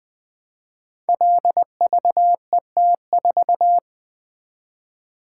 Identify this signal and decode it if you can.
Morse code, heard as a keyed tone, about 20 wpm, 700 Hz. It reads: LVET4
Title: Code Groups - 5 Letter:Numbers - 20wpm